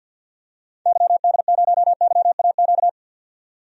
Morse code, heard as a keyed tone, about 52 wpm, 700 Hz. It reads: YD0YAQ
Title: DXCC-100 Call Signs - 6 Characters - 50wpm